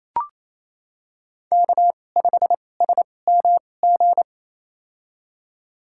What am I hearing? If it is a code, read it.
Morse code, heard as a keyed tone, about 28 wpm, spaced as for 20 wpm, 700 Hz. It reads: K5SMG
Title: USA Call Signs - 5 Characters - 2x Character Spacing - 28wpm